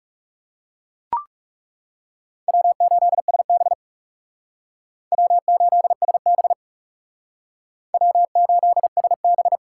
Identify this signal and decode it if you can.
Morse code, heard as a keyed tone, about 40 wpm, 700 Hz. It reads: W8SB W8SB W8SB